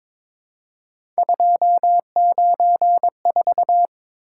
Morse code, heard as a keyed tone, about 22 wpm, 700 Hz. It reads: 294
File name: practice / 22wpm / Three Letter-Number Combinations - 22wpm